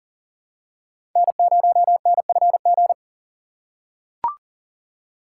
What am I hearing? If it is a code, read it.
Morse code, heard as a keyed tone, about 40 wpm, 700 Hz. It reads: N0NFG
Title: USA Call Signs - 5 Characters - 2x Character Spacing - 40wpm